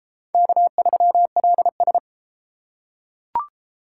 Morse code, heard as a keyed tone, about 32 wpm, 700 Hz. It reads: K3LS